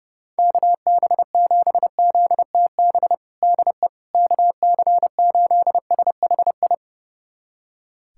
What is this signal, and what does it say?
Morse code, heard as a keyed tone, about 32 wpm, 700 Hz. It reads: KB7ZTB DE KC8SHI
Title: USA Callsign DE Callsign - 6 Characters - 30wpm